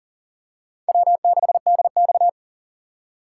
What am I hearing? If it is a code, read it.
Morse code, heard as a keyed tone, about 40 wpm, 700 Hz. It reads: W6DX
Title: USA Call Signs - 4 Characters - 3x Character Spacing - 40wpm